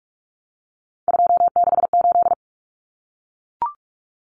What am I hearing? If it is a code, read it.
Morse code, heard as a keyed tone, about 44 wpm, 700 Hz. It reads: 268
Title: Common Cut Numbers - 3-Digit 45wpm